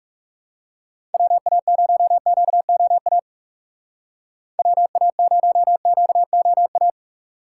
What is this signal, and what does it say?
Morse code, heard as a keyed tone, about 35 wpm, 700 Hz. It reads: WA0QOA WA0QOA
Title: Speed Racing - USA 2x3 Call Signs 30wpm